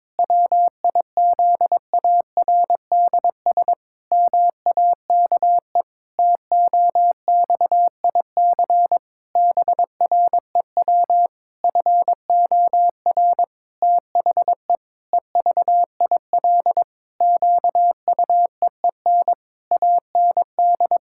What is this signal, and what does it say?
Morse code, heard as a keyed tone, about 22 wpm, 700 Hz. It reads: WIZARDS MAKE TOXIC BREW FOR THE EVIL QUEEN AND
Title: Pangram Challenge v2 - 22wpm